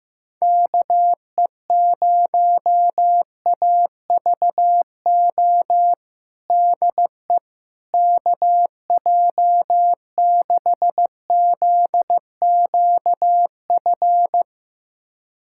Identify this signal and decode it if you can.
Morse code, heard as a keyed tone, about 15 wpm, 700 Hz. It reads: KE0AVO DE KJ6ZQF
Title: USA Callsign DE Callsign 15wpm